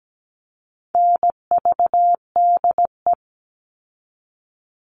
Morse code, heard as a keyed tone, about 17 wpm, 700 Hz. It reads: NVDE
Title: All Cut Numbers - 4-Digit 17wpm